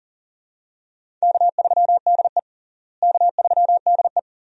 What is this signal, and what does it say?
Morse code, heard as a keyed tone, about 40 wpm, 700 Hz. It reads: K3DE K3DE
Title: USA Call Signs - 4 Characters - Repeated 2x - 40wpm